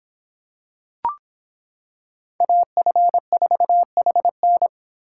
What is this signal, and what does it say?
Morse code, heard as a keyed tone, about 25 wpm, 700 Hz. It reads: AF4HN